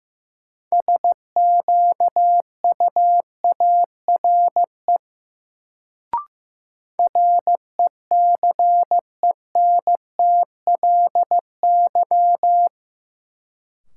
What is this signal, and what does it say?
Morse code, heard as a keyed tone, about 16 wpm, 700 Hz. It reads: SQUARE RECENTLY
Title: Sentences from Top 1200 Words - New Word Drill Alt - 15wpm